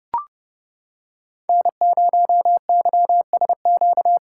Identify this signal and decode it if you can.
Morse code, heard as a keyed tone, about 30 wpm, 700 Hz. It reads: N0YSQ
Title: K1USN SST QSO Practice - 30wpm